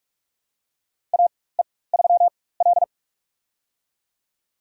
Morse code, heard as a keyed tone, about 45 wpm, spaced as for 21 wpm, 700 Hz. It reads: AE3L
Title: USA Call Signs - 4 Characters - 4x Character Spacing - 45wpm